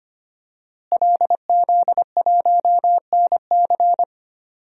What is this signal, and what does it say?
Morse code, heard as a keyed tone, about 25 wpm, 700 Hz. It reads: LZ1NC
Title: DXCC-100 Call Signs - 5 Characters - Repeated 2x - 25wpm